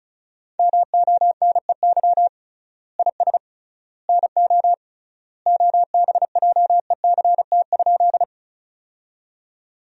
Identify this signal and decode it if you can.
Morse code, heard as a keyed tone, about 36 wpm, 700 Hz. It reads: MONEY IS NO OBJECT?
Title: 3x Word-Spacing - Sentences from Top 500 Words - 35wpm